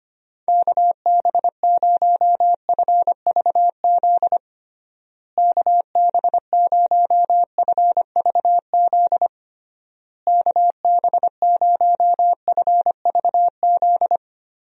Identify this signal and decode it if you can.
Morse code, heard as a keyed tone, about 25 wpm, 700 Hz. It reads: KB0FVZ KB0FVZ KB0FVZ
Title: USA Call Signs - 6 Characters - Repeated 3x - 25wpm